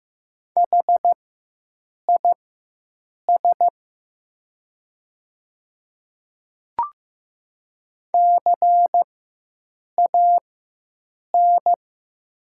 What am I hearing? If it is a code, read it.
Morse code, heard as a keyed tone, about 15 wpm, spaced as for 7 wpm, 700 Hz. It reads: HIS CAN
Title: 4x Character Spacing - Top 100 Words - 15wpm